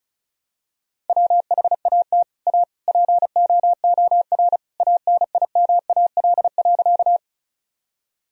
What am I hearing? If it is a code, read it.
Morse code, heard as a keyed tone, about 36 wpm, 700 Hz. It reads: WHAT A POOR ANIMAL.
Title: Sentences from Top 1000 Words - Review Part 1 - 35wpm